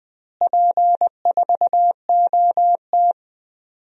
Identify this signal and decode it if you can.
Morse code, heard as a keyed tone, about 21 wpm, 700 Hz. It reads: P4OT
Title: DXCC-100 Call Signs - 4 Characters - Repeated 2x - 20wpm